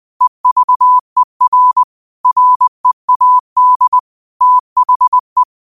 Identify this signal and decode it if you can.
Morse code, heard as a keyed tone, about 20 wpm, 1000 Hz. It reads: EVER READ THE
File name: qotdfiles / 20wpm